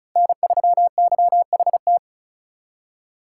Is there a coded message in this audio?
Yes: Morse code from 0.2 to 2.0 s, heard as a keyed tone, about 45 wpm, 700 Hz. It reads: N3YHT